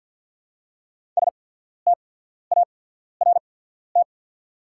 Morse code, heard as a keyed tone, about 49 wpm, spaced as for 14 wpm, 690 Hz. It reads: START
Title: 8x Character Spacing - Top 500 Words - 50wpm